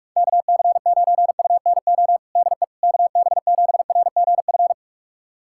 Morse code, heard as a keyed tone, about 47 wpm, 700 Hz. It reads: KC9UNO DE KB7RGF
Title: USA Callsign DE Callsign 45wpm